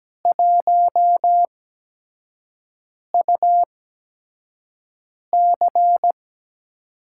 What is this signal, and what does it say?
Morse code, heard as a keyed tone, about 17 wpm, spaced as for 5 wpm, 700 Hz. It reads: 1UC
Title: USA Call Signs - 5 Characters - 8x Character Spacing - 17wpm